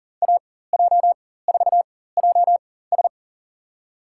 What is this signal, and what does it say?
Morse code, heard as a keyed tone, about 40 wpm, spaced as for 19 wpm, 700 Hz. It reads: AJ4JS
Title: USA Call Signs - 5 Characters - 4x Character Spacing - 40wpm